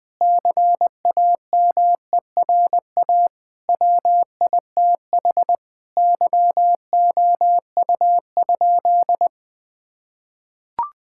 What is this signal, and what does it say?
Morse code, heard as a keyed tone, about 21 wpm, 700 Hz. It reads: CAMERA WITH YOU?
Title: Sentences from Top 1000 Words - Review Part 5 - 20wpm